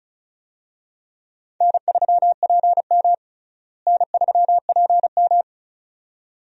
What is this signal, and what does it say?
Morse code, heard as a keyed tone, about 34 wpm, 700 Hz. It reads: N3PM N3PM
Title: USA Call Signs - 4 Characters - Repeated 2x - 35wpm